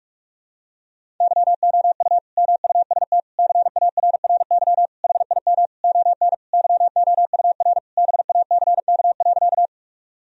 Morse code, heard as a keyed tone, about 45 wpm, 700 Hz. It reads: YOU MUST CARRY HIM ON YOUR BACK.
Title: Sentences from Top 1000 Words - Review Part 1 - 45wpm